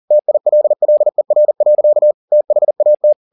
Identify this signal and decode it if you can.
Morse code, heard as a keyed tone, about 40 wpm, 600 Hz. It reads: TILLER. THAT
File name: the old man and the sea 2 40wpm